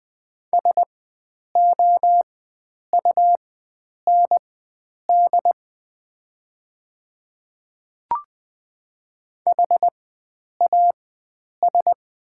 Morse code, heard as a keyed tone, about 20 wpm, spaced as for 9 wpm, 700 Hz. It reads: SOUND HAS
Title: 4x Character Spacing - Top 100 Words - 20wpm